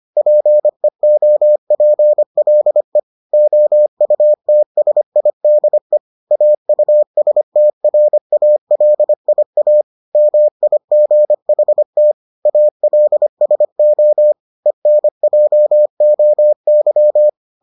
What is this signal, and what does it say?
Morse code, heard as a keyed tone, about 25 wpm, 600 Hz. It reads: PEOPLE OUTSIDE AUSTRALIA MIGHT ALSO ENJOY